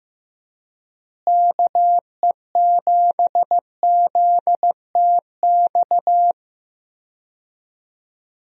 Morse code, heard as a keyed tone, about 15 wpm, 700 Hz. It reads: KE7ZTX
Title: USA Call Signs - 6 Characters - Repeated 3x - 15wpm